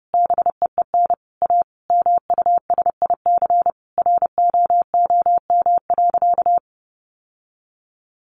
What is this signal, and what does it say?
Morse code, heard as a keyed tone, about 30 wpm, 700 Hz. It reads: BEEN A MUSIC ROOM.
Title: Sentences from Top 700 Words 30wpm